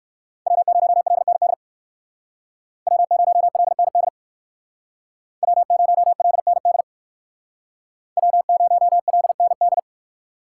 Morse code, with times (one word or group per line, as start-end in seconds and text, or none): 0.5-1.5 W0LND
2.9-4.1 W0LND
5.4-6.8 W0LND
8.2-9.8 W0LND